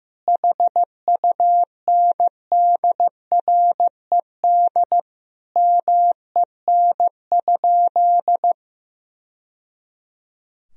Morse code, heard as a keyed tone, about 16 wpm, 700 Hz. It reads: HUNDRED MEN?
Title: Sentences from Top 1000 Words - Review Part 2 - 15wpm